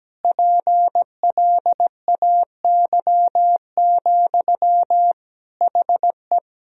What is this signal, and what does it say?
Morse code, heard as a keyed tone, about 17 wpm, 700 Hz. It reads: PLAY, HE
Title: Sentences from Top 1000 Words - Review Part 4 - 17wpm